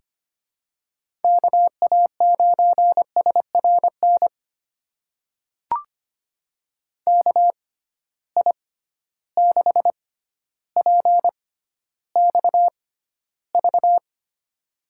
Morse code, heard as a keyed tone, about 25 wpm, spaced as for 13 wpm, 700 Hz. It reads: KA9SRN KI6PXV
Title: USA Call Signs - 6 Characters - 6x Character Spacing - 25wpm